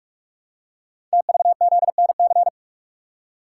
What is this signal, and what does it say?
Morse code, heard as a keyed tone, about 44 wpm, 700 Hz. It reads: TVZNC
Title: Code Groups - 5 Letter:Numbers - 45wpm